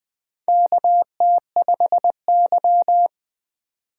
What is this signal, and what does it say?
Morse code, heard as a keyed tone, about 21 wpm, 700 Hz. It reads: KT5Y
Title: USA Call Signs - 4 Characters - 8x Character Spacing - 20wpm